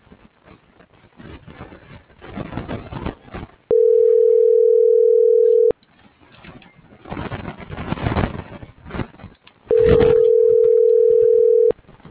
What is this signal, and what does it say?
Ringback tone (North America), 440 + 480 Hz.
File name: CENTRAL GENERAL / out tania.lemus - .459